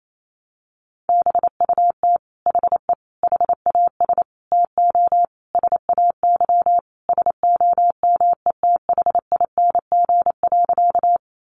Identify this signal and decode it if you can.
Morse code, heard as a keyed tone, about 28 wpm, 700 Hz. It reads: BUT HE HAS TO SAY SOMETHING.
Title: Sentences from Top 600 Words 28wpm